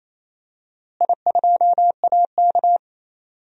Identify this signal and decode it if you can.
Morse code, heard as a keyed tone, about 28 wpm, 700 Hz. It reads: I2AK
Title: DXCC-100 Call Signs - 4 Characters - Repeated 3x - 28wpm